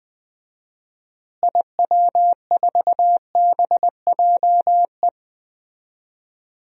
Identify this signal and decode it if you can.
Morse code, heard as a keyed tone, about 20 wpm, 700 Hz. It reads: IW4BJE
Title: DXCC-100 Call Signs - 6 Characters - Repeated 2x - 20wpm